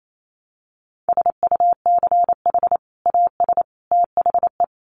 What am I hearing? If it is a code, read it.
Morse code, heard as a keyed tone, about 28 wpm, 700 Hz. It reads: SUCH AS THE
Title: Sets of 3 Words 28wpm